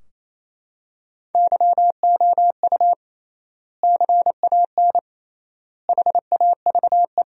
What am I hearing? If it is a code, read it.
Morse code, heard as a keyed tone, about 28 wpm, 700 Hz. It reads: YOU CAN HAVE
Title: 3x Word-Spacing - Sentences from Top 100 Words - 28wpm